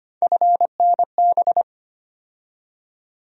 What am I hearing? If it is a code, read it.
Morse code, heard as a keyed tone, about 25 wpm, 700 Hz. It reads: FNB
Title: USA Call Sign Postfixes - 3 Letters 25wpm